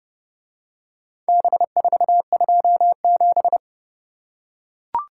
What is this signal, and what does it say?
Morse code, heard as a keyed tone, about 30 wpm, 700 Hz. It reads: B427